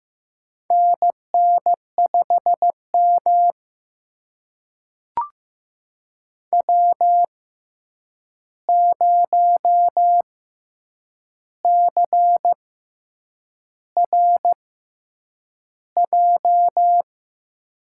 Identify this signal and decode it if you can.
Morse code, heard as a keyed tone, about 15 wpm, spaced as for 7 wpm, 700 Hz. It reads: NN5M W0CRJ
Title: USA Call Signs - 6x Character Spacing - 15wpm